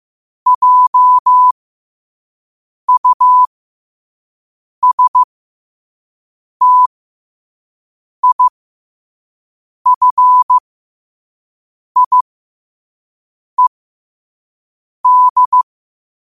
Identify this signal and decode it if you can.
Morse code, heard as a keyed tone, about 15 wpm, spaced as for 5 wpm, 1000 Hz. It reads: JUSTIFIED
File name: qotdfiles / 5wpm